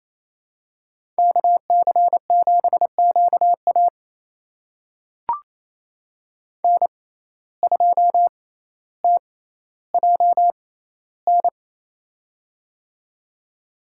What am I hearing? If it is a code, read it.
Morse code, heard as a keyed tone, about 27 wpm, spaced as for 14 wpm, 700 Hz. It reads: KC7QA N2TJN
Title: USA Call Signs - 6x Character Spacing - 28wpm